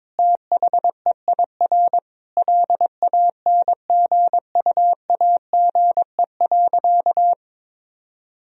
Morse code, heard as a keyed tone, about 22 wpm, 700 Hz. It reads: THEIR LANGUAGE.
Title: Sentences from Top 1000 Words - Review Part 3 - 22wpm